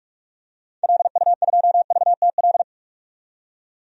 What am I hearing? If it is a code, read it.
Morse code, heard as a keyed tone, about 46 wpm, 690 Hz. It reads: LU2VTL
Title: DXCC-100 Call Signs - 6 Characters - 45wpm